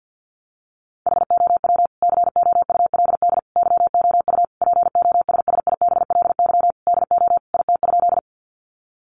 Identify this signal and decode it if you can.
Morse code, heard as a keyed tone, about 49 wpm, 700 Hz. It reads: HOW COULD YOU POSSIBLY DO IT?
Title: Sentences from Top 1300 Words 50wpm